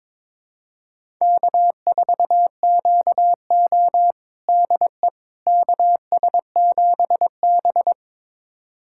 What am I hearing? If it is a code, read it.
Morse code, heard as a keyed tone, about 22 wpm, 700 Hz. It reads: K4QO DE KS7B
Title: USA Callsign DE Callsign - 4 Characters - 22wpm